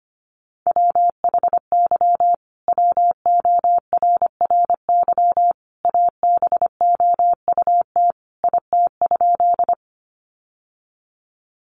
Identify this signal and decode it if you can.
Morse code, heard as a keyed tone, about 25 wpm, 700 Hz. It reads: WHY WORRY ABOUT IT?
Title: Sentences from Top 1100 Words 25wpm